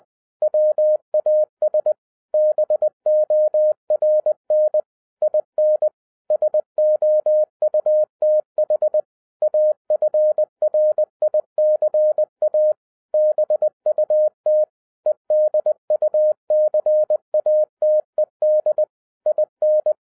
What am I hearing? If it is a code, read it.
Morse code, heard as a keyed tone, about 20 wpm, 610 Hz. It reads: WAS BORN IN SOUTH AFRICA BUT EDUCATED IN